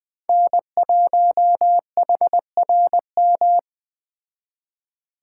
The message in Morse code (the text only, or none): N1HRM